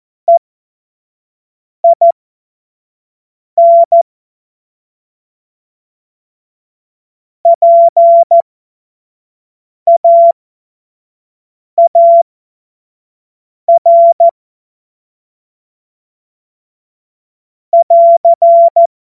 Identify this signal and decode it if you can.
Morse code, heard as a keyed tone, about 14 wpm, spaced as for 5 wpm, 680 Hz. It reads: EIN PAAR <AR>
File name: Morsekurs / Witze / Witz 21. Februar 2023 25 F70